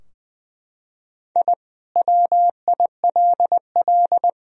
Morse code, heard as a keyed tone, about 21 wpm, 700 Hz. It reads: I WILL
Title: Sentences from Top 1000 Words - Review Part 3 - 20wpm